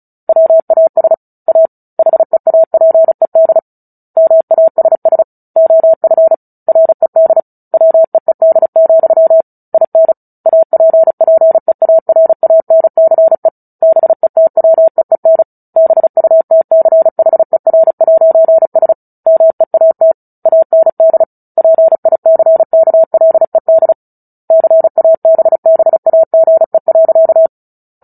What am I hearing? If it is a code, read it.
Morse code, heard as a keyed tone, about 34 wpm, 650 Hz. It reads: WAS A HEAPED MASS OF RED WEED, IN APPEARANCE BETWEEN BUTCHER'S MEAT AND PICKLED CABBAGE.
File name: Book Two / 09-Chapter 9 - 35 WPM